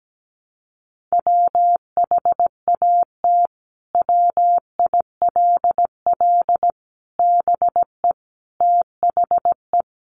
Morse code, heard as a keyed tone, about 17 wpm, 700 Hz. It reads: WHAT WILL BE THE
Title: Sentences from Top 700 Words 17wpm